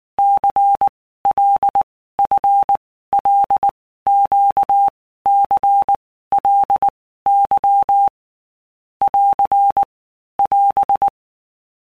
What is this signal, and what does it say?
Morse code, heard as a keyed tone, about 19 wpm, 800 Hz. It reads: C L F L Q C L Y <AR> <AS>
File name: Basic Letter Groups By Blocks / Groups@18 / LFCYQ@18